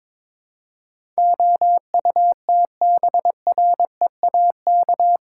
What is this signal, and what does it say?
Morse code, heard as a keyed tone, about 22 wpm, 700 Hz. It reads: OUTBREAK
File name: practice / 22wpm / Eight Letter Words - 22wpm